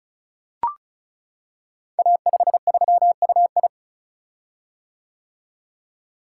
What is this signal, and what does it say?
Morse code, heard as a keyed tone, about 36 wpm, 690 Hz. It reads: A53UI
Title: Code Groups - 5 Letter:Numbers - 35wpm